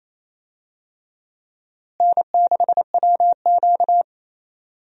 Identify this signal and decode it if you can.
Morse code, heard as a keyed tone, about 28 wpm, 700 Hz. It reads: N6WQ
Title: USA Call Signs - 4 Characters - Repeated 3x - 28wpm